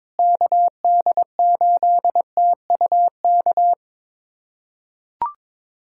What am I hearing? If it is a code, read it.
Morse code, heard as a keyed tone, about 22 wpm, 700 Hz. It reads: KD8TUK
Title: USA Callsign DE Callsign 22wpm